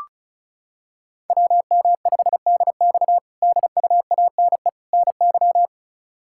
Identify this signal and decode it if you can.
Morse code, heard as a keyed tone, about 34 wpm, 700 Hz. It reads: WM5DX DUANE NY
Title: K1USN SST QSO Practice - 35wpm